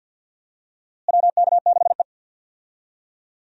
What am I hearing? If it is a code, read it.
Morse code, heard as a keyed tone, about 47 wpm, 700 Hz. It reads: WK6E
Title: USA Call Signs - 4 Characters - Repeated 3x - 50wpm